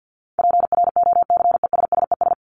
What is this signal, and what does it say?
Morse code, heard as a keyed tone, about 50 wpm, 700 Hz. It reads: PROCESSES
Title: ES:S Suffix - Top 100 Words with Base Words 50wpm